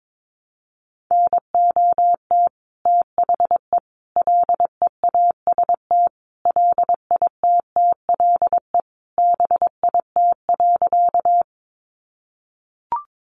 Morse code, heard as a keyed tone, about 22 wpm, 700 Hz. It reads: NOT THE LEAST LITTLE BIT.
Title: Sentences from Top 900 Words 22wpm